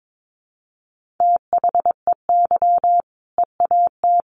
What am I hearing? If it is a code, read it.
Morse code, heard as a keyed tone, about 22 wpm, 700 Hz. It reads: THEY EAT
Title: Easy Phrases 22wpm